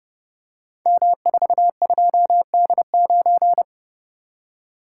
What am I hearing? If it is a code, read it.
Morse code, heard as a keyed tone, about 30 wpm, 700 Hz. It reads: M42D9
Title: Code Groups - 5 Letter:Numbers - 30wpm